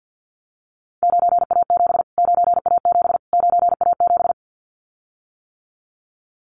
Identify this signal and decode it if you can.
Morse code, heard as a keyed tone, about 50 wpm, 700 Hz. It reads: 9A7 9A7 9A7
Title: Callsign Prefixes for Top 25 DXCC - Repeated 3x - 50wpm